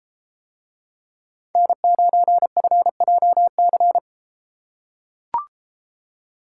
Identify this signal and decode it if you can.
Morse code, heard as a keyed tone, about 33 wpm, 700 Hz. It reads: N9FJC